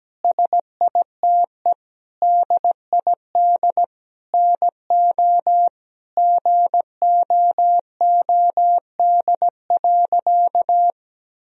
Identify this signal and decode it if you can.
Morse code, heard as a keyed tone, about 17 wpm, 700 Hz. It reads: SITE DID NO GOOD.
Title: Sentences from Top 1000 Words - Review Part 4 - 17wpm